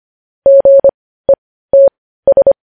Morse code, heard as a keyed tone, about 25 wpm, spaced as for 15 wpm, 550 Hz. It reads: GETS
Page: news headlines in morse code at 15 wpm